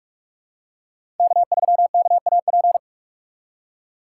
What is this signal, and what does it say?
Morse code, heard as a keyed tone, about 45 wpm, 700 Hz. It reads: K3KAP
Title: USA Call Signs - 5 Characters - 6x Character Spacing - 45wpm